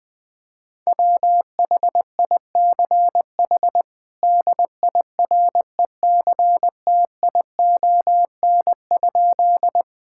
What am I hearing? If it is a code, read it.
Morse code, heard as a keyed tone, about 19 wpm, 700 Hz. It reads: WHICH DIRECTION?
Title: Sentences from Top 1000 Words - Review Part 5 - 20wpm